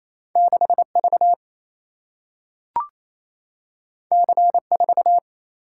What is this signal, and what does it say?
Morse code, heard as a keyed tone, about 28 wpm, 700 Hz. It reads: 6V C4
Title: Single-Letter Single-Number 28wpm